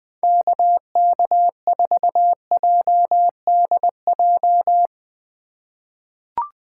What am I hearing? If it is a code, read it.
Morse code, heard as a keyed tone, about 20 wpm, 700 Hz. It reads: KK4JDJ
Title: USA Call Signs - 4x Character Spacing - 20wpm